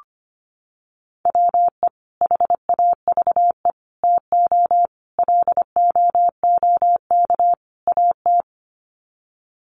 Morse code, heard as a keyed tone, about 25 wpm, 700 Hz. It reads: WE HAVE TO LOOK AT
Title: Sets of 5 Words Encore2 25wpm